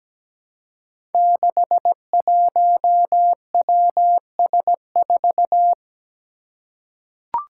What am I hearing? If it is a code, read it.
Morse code, heard as a keyed tone, about 17 wpm, 700 Hz. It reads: 61WS4
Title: Five Challenging Letter-Number Combinations 17wpm